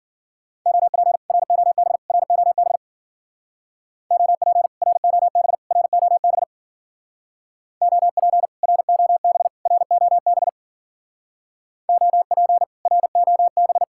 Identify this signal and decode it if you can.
Morse code, heard as a keyed tone, about 52 wpm, 700 Hz. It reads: OP ROB ROB OP ROB ROB OP ROB ROB OP ROB